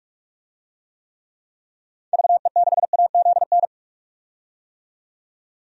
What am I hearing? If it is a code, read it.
Morse code, heard as a keyed tone, about 46 wpm, 700 Hz. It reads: VE6AZN
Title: DXCC-100 Call Signs - 6 Characters - 45wpm